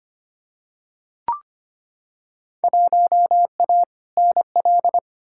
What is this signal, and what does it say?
Morse code, heard as a keyed tone, about 25 wpm, 700 Hz. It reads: 1A NL
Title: Field Day Exchanges Encore - 25wpm